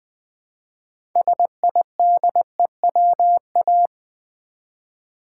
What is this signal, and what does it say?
Morse code, heard as a keyed tone, about 20 wpm, 700 Hz. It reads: SIDEWA
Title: Build A Word - 2 Letters At A Time - 8-Letter Words - 20wpm